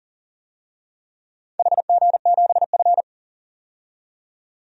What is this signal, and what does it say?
Morse code, heard as a keyed tone, about 42 wpm, 700 Hz. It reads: HG7F
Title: DXCC-100 Call Signs - 4 Characters - 40wpm